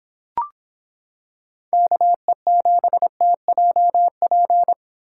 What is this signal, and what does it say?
Morse code, heard as a keyed tone, about 26 wpm, 700 Hz. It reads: KE7TJP